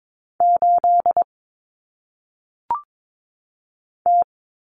Morse code, heard as a keyed tone, about 22 wpm, 700 Hz. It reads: 8 T
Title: Common Cut Numbers - 1-Digit 22wpm